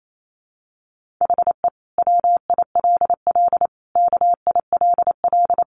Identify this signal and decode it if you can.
Morse code, heard as a keyed tone, about 28 wpm, 700 Hz. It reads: HE WILL KILL
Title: Sentences from Top 1400 Words 28wpm